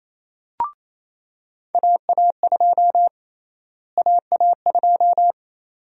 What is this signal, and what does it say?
Morse code, heard as a keyed tone, about 28 wpm, 700 Hz. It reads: AA2 AA2
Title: USA Call Sign Prefixes - Repeated 2x - 28wpm